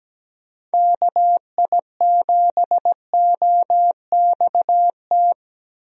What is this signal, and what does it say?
Morse code, heard as a keyed tone, about 17 wpm, 700 Hz. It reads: KI7OXT
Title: USA Call Signs - 6 Characters - Repeated 2x - 17wpm